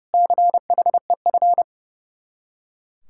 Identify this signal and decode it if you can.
Morse code, heard as a keyed tone, about 37 wpm, 700 Hz. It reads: CHEF
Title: Speed-Racing Common QSO Phrases Encore - 30wpm